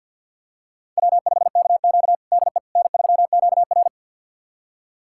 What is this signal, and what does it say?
Morse code, heard as a keyed tone, about 51 wpm, 700 Hz. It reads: W5KQ DE N3QR
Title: USA Callsign DE Callsign - 4 Characters - 50wpm